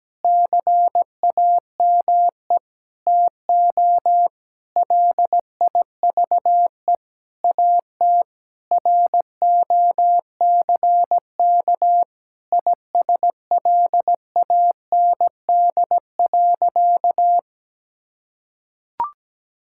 Morse code, heard as a keyed tone, about 17 wpm, 700 Hz. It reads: CAME TO LIVE AT ROCK ISLAND.
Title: Sentences from Top 1600 Words 17wpm